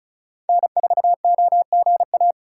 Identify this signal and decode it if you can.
Morse code, heard as a keyed tone, about 36 wpm, 700 Hz. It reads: N4OGA
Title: USA Call Signs - 5 Characters - 4x Character Spacing - 35wpm